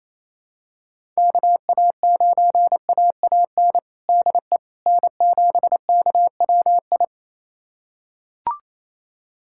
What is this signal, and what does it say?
Morse code, heard as a keyed tone, about 28 wpm, 700 Hz. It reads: KA9AAN DE N7KWI
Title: USA Callsign DE Callsign 28wpm